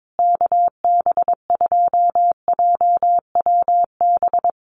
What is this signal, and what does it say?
Morse code, heard as a keyed tone, about 22 wpm, 700 Hz. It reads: KB2JWB
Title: USA Call Signs Repeated 3x 22wpm